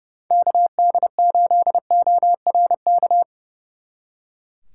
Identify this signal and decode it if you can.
Morse code, heard as a keyed tone, about 38 wpm, 700 Hz. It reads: KD8ORK